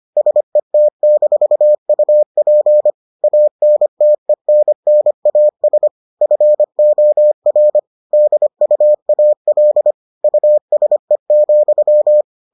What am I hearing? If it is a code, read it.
Morse code, heard as a keyed tone, about 25 wpm, 600 Hz. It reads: SET-UP ANTENNAS FOR DUAL USE,